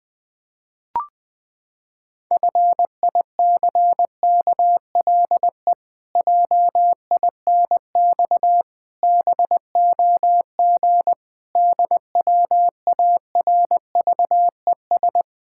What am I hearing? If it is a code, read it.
Morse code, heard as a keyed tone, about 20 wpm, 700 Hz. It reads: FICKLE JINX BOG DWARVES
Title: Pangram Challenge v2 - 20wpm